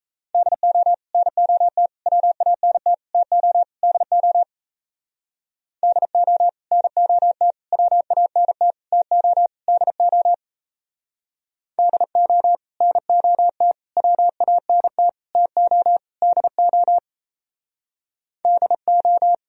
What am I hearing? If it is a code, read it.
Morse code, heard as a keyed tone, about 36 wpm, 700 Hz. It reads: DO NOT WANT TO DO DO NOT WANT TO DO DO NOT WANT TO DO DO